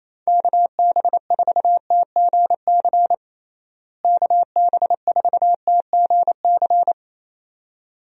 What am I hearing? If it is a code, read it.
Morse code, heard as a keyed tone, about 28 wpm, 700 Hz. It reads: KB4TGC KB4TGC
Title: USA Call Signs - 6 Characters - Repeated 2x - 28wpm